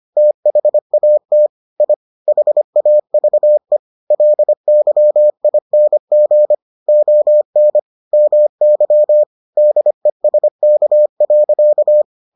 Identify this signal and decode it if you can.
Morse code, heard as a keyed tone, about 25 wpm, 600 Hz. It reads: THAT I HAVE LYING ON MY DESK.